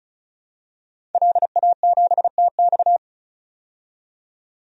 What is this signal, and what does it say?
Morse code, heard as a keyed tone, about 36 wpm, 700 Hz. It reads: LA7TX
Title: DXCC-100 Call Signs - 5 Characters - 35wpm